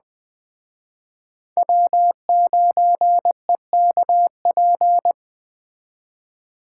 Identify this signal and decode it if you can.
Morse code, heard as a keyed tone, about 20 wpm, 700 Hz. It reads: W9EKP
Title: Speed Racing - USA 1x3 Call Signs 17wpm